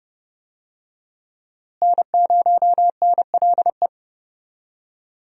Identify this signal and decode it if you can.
Morse code, heard as a keyed tone, about 30 wpm, 700 Hz. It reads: N0NLE
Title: USA Call Signs - 5 Characters - Repeated 3x - 30wpm